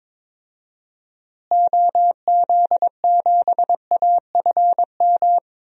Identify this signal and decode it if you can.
Morse code, heard as a keyed tone, about 22 wpm, 700 Hz. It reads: OZ7AFM
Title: DXCC-100 Call Signs - 6 Characters - Repeated 3x - 22wpm